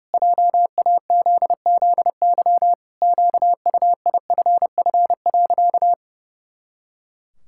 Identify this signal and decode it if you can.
Morse code, heard as a keyed tone, about 30 wpm, 700 Hz. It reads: JAZZY QUIFF.
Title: Pangram Challenge v2 - 30wpm